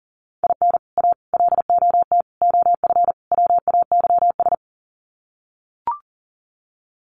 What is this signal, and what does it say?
Morse code, heard as a keyed tone, about 40 wpm, 700 Hz. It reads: IN A LOT OF WAYS
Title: Sets of 5 Words Encore 40wpm